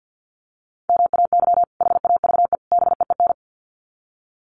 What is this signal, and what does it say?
Morse code, heard as a keyed tone, about 50 wpm, 700 Hz. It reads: MAY HAVE BEEN
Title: Sets of 3 Words 50wpm